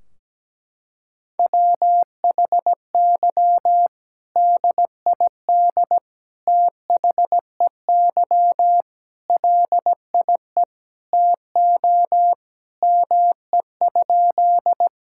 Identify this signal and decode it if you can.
Morse code, heard as a keyed tone, about 17 wpm, 700 Hz. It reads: WHY DID THEY LIE TO ME?
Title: Sentences from Top 1600 Words 17wpm